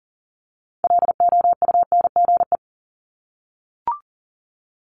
Morse code, heard as a keyed tone, about 37 wpm, 700 Hz. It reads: LOUNGE
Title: Multi-letter Phonemes 40wpm